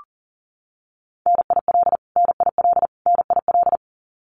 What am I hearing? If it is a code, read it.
Morse code, heard as a keyed tone, about 40 wpm, 700 Hz. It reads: NIL NIL NIL
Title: Abbreviations Repeated 3x - 40wpm